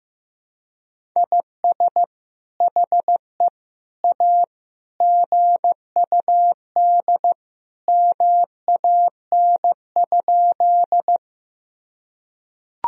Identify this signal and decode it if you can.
Morse code, heard as a keyed tone, about 15 wpm, 700 Hz. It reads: IS HE A GUD MAN?
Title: Sentences from Top 200 Words and Abbreviations 15wpm